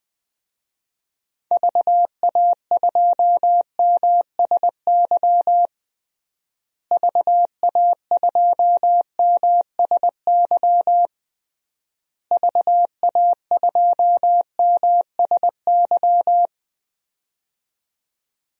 Morse code, heard as a keyed tone, about 20 wpm, 700 Hz. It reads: VA2MSY VA2MSY VA2MSY
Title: DXCC-100 Call Signs - 6 Characters - Repeated 3x - 20wpm